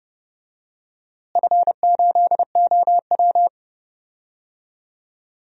Morse code, heard as a keyed tone, about 30 wpm, 700 Hz. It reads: F8OW
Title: DXCC-100 Call Signs - 4 Characters - Repeated 3x - 30wpm